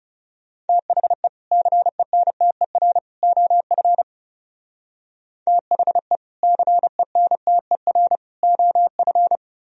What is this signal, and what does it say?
Morse code, heard as a keyed tone, about 36 wpm, 700 Hz. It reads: THE CENTER OF THE CENTER OF
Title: Speed-Racing - Sets of 3 Words 30wpm